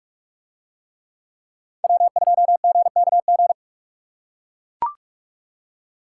Morse code, heard as a keyed tone, about 44 wpm, 700 Hz. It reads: W2GKG